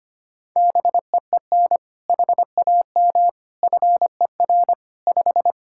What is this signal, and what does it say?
Morse code, heard as a keyed tone, about 25 wpm, 700 Hz. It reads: BEEN HAM FER 5
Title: Practice QSO - Exchange 3 - More Info - 25wpm